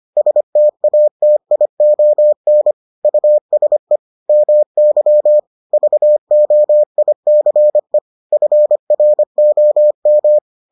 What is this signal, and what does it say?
Morse code, heard as a keyed tone, about 25 wpm, 600 Hz. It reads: STATION USE MY VOICE FROM